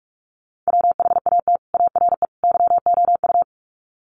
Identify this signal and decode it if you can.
Morse code, heard as a keyed tone, about 44 wpm, 700 Hz. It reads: WHAT ARE YOU
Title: Sets of 3 Words 45wpm